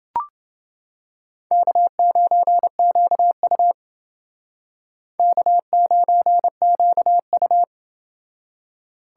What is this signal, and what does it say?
Morse code, heard as a keyed tone, about 28 wpm, 700 Hz. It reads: K9QU K9QU